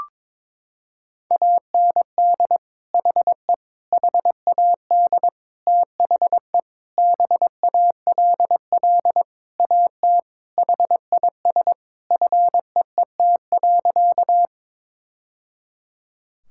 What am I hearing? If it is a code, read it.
Morse code, heard as a keyed tone, about 22 wpm, 700 Hz. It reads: AND HE HAD THE BALL AT HIS FEET.
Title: Sentences from Top 1000 Words - Review Part 6 - 22wpm